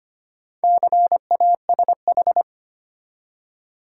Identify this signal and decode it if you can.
Morse code, heard as a keyed tone, about 25 wpm, 700 Hz. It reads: CASH
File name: practice / 25wpm / Easy Words with Hints - 25wpm